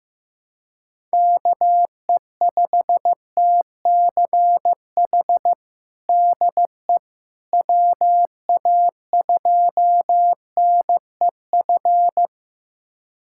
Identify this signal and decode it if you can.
Morse code, heard as a keyed tone, about 15 wpm, 700 Hz. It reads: KE5TCH DE WA2NEF
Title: USA Callsign DE Callsign 15wpm